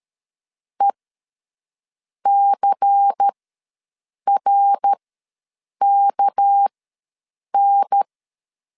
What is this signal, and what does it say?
Morse code, heard as a keyed tone, about 13 wpm, spaced as for 6 wpm, 800 Hz. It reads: ECRKN
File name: LEZIONI / 6 PKX0 / 6 PK ws 12wpm 3116